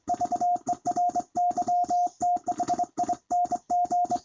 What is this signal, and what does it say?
Morse code, heard as a keyed tone, about 22 wpm, 710 Hz. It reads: VERYTHING
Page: NW6V sent me a recording of his doing 25 wpm on a straight key.